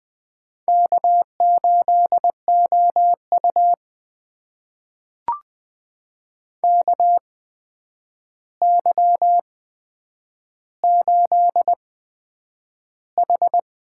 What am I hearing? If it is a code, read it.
Morse code, heard as a keyed tone, about 20 wpm, spaced as for 9 wpm, 700 Hz. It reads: K8OU KY8H
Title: USA Call Signs - 4 Characters - 8x Character Spacing - 20wpm